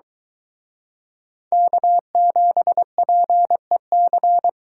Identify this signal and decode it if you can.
Morse code, heard as a keyed tone, about 23 wpm, 700 Hz. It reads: K7PEC